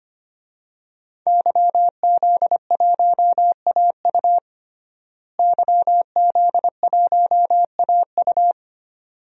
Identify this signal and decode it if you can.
Morse code, heard as a keyed tone, about 25 wpm, 700 Hz. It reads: YZ1AU YZ1AU
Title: DXCC-100 Call Signs - 5 Characters - Repeated 2x - 25wpm